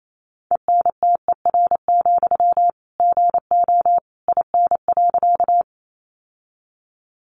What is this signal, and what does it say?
Morse code, heard as a keyed tone, about 28 wpm, 700 Hz. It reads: ENTER, GO IN.
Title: Sentences from Top 1300 Words 28wpm